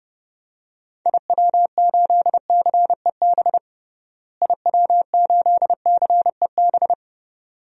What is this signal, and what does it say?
Morse code, heard as a keyed tone, about 30 wpm, 700 Hz. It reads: IW8CEB IW8CEB
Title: DXCC-100 Call Signs - 6 Characters - Repeated 2x - 30wpm